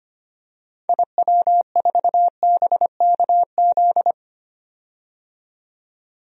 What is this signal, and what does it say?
Morse code, heard as a keyed tone, about 25 wpm, 700 Hz. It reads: IW4BKZ
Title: DXCC-100 Call Signs - 6 Characters - 25wpm